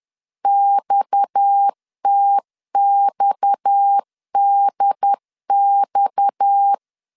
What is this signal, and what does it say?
Morse code, heard as a keyed tone, about 10 wpm, 800 Hz. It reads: XTXDX
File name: LEZIONI / 6 PKX0 / 6 PKX 10wpm 2426